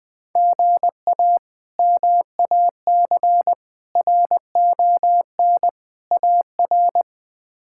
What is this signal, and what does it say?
Morse code, heard as a keyed tone, about 20 wpm, 700 Hz. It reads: GA MAC RON AR
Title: K1USN SST QSO Practice - 20wpm